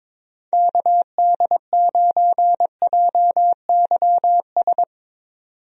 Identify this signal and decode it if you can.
Morse code, heard as a keyed tone, about 22 wpm, 700 Hz. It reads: KD9JYS
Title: USA Call Signs - 6 Characters - 22wpm